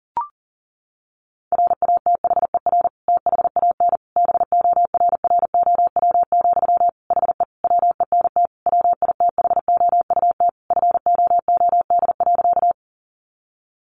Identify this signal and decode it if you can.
Morse code, heard as a keyed tone, about 40 wpm, 700 Hz. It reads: RATHER THAN BORROW, HE WENT WITHOUT FOOD.
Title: Sentences from Top 1500 Words 40wpm